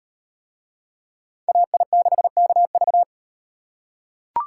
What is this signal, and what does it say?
Morse code, heard as a keyed tone, about 38 wpm, 700 Hz. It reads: AI6KV